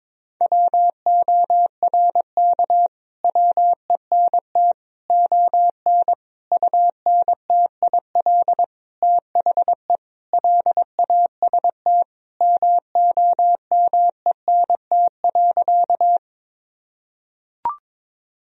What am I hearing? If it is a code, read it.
Morse code, heard as a keyed tone, about 22 wpm, 700 Hz. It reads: WORK WENT ON UNTIL THE LAST MOMENT.
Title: Sentences from Top 1000 Words - Review Part 1 - 22wpm